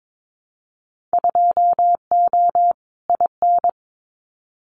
Morse code, heard as a keyed tone, about 22 wpm, 700 Hz. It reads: 2O IN
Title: Winter Field Day Exchanges 22wpm